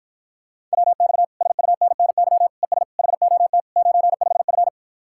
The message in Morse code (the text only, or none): WX SUNNY ES HOT 95F